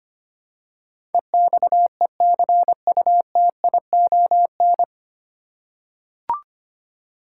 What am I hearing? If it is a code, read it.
Morse code, heard as a keyed tone, about 25 wpm, 700 Hz. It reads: EXECUTION
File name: Nine Letter Words - 25wpm